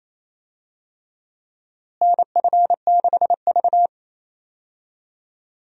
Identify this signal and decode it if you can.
Morse code, heard as a keyed tone, about 28 wpm, 700 Hz. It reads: NF6V
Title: USA Call Signs - 4 Characters - Repeated 3x - 28wpm